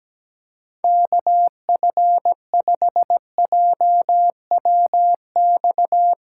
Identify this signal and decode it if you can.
Morse code, heard as a keyed tone, about 17 wpm, 700 Hz. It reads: KF5JWX
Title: USA Callsign DE Callsign 17wpm